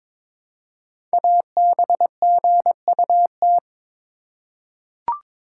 Morse code, heard as a keyed tone, about 22 wpm, 700 Hz. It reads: ABGUT
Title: Code Groups - 5 Letters - 22wpm